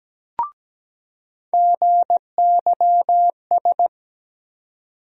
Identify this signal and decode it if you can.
Morse code, heard as a keyed tone, about 17 wpm, 700 Hz. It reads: GYS
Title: USA Call Sign Postfixes - 3 Letters 17wpm